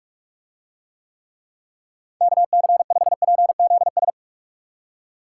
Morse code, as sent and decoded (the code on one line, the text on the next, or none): -.- -.-. ..... .--. --.. ...
KC5PZS